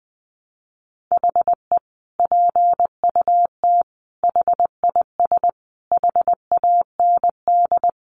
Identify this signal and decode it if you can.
Morse code, heard as a keyed tone, about 20 wpm, 700 Hz. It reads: HE PUT HIS HAND
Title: Sentences from Top 400 Words 20wpm